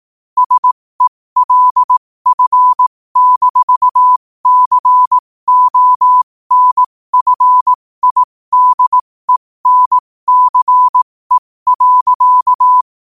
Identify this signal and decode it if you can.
Morse code, heard as a keyed tone, about 18 wpm, spaced as for 15 wpm, 1000 Hz. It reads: SELF-CONFIDENCE.
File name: 15wpm